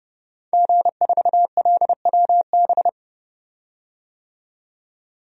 Morse code, heard as a keyed tone, about 30 wpm, 700 Hz. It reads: G4LWB
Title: DXCC-100 Call Signs - 5 Characters - 30wpm